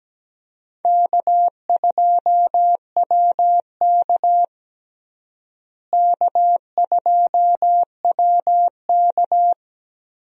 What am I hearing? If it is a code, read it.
Morse code, heard as a keyed tone, about 17 wpm, 700 Hz. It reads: K2WK K2WK
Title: USA Call Signs - 4 Characters - Repeated 3x - 17wpm